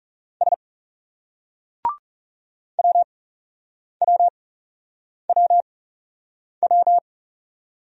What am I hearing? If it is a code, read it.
Morse code, heard as a keyed tone, about 40 wpm, spaced as for 8 wpm, 700 Hz. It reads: S WWWW